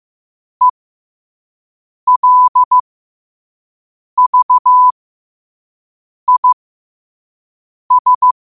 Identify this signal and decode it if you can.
Morse code, heard as a keyed tone, about 15 wpm, spaced as for 5 wpm, 1000 Hz. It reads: ELVIS
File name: qotdfiles / 5wpm